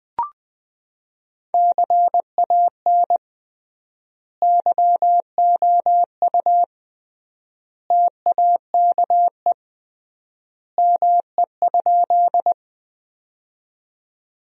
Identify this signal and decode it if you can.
Morse code, heard as a keyed tone, about 20 wpm, 700 Hz. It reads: CAN YOU TAKE ME?
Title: 3x Word-Spacing - Sentences from Top 200 Words - 20wpm